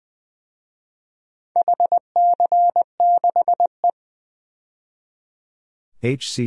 Morse code, heard as a keyed tone, about 21 wpm, 700 Hz. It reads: HC6E
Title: DXCC-100 Call Signs - 4 Characters - 20wpm